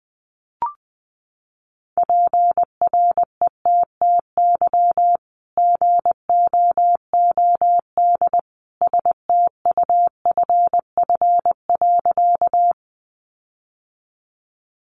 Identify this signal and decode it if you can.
Morse code, heard as a keyed tone, about 20 wpm, 700 Hz. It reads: PRETTY GOOD STUFF.
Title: Sentences from Top 1400 Words 20wpm